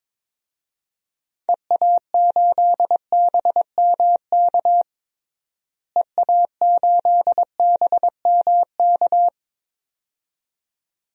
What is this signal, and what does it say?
Morse code, heard as a keyed tone, about 22 wpm, 700 Hz. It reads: EA8BMK EA8BMK
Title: DXCC-100 Call Signs - 6 Characters - Repeated 2x - 22wpm